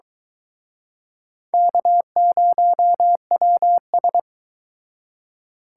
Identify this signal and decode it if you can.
Morse code, heard as a keyed tone, about 24 wpm, 700 Hz. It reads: K0WS